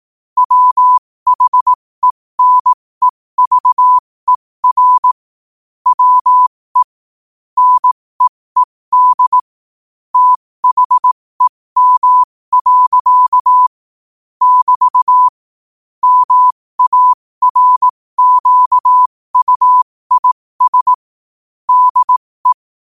Morse code, heard as a keyed tone, about 18 wpm, spaced as for 15 wpm, 1000 Hz. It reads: WHENEVER WE NEED THEM. <BT> MARQUIS DE
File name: qotdfiles / 15wpm